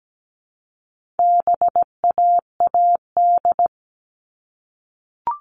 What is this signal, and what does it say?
Morse code, heard as a keyed tone, about 17 wpm, 700 Hz. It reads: BAAD
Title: All Cut Numbers - 4-Digit 17wpm